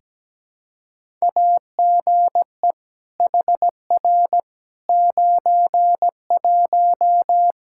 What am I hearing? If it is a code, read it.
Morse code, heard as a keyed tone, about 17 wpm, 700 Hz. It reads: AGE HR 91
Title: K1USN SST QSO Practice 17wpm